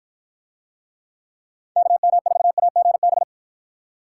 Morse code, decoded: KM4AGB

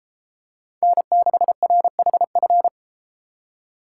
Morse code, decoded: N6RHF

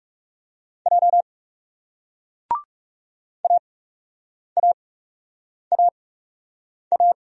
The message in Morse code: J AAAA

